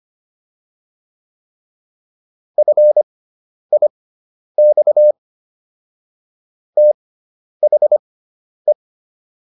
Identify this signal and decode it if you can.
Morse code, heard as a keyed tone, about 25 wpm, spaced as for 10 wpm, 600 Hz. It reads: FIX THE